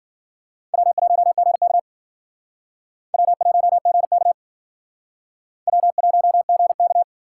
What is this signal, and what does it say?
Morse code, heard as a keyed tone, about 52 wpm, 700 Hz. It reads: W1GK W1GK W1GK